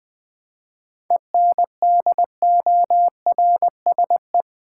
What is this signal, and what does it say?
Morse code, heard as a keyed tone, about 20 wpm, 700 Hz. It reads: ENDORSE
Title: 7 Letter Words - First 3 Letters Given - 20wpm